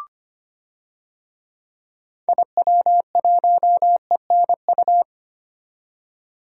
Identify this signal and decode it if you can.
Morse code, heard as a keyed tone, about 25 wpm, 700 Hz. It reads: IW1ENU
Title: DXCC-100 Call Signs - 6 Characters - 25wpm